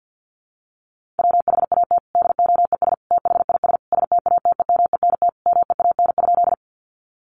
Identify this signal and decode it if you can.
Morse code, heard as a keyed tone, about 48 wpm, 700 Hz. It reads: WHAT DOES THIS STATEMENT MEAN?
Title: Sentences from Top 1200 Words 50wpm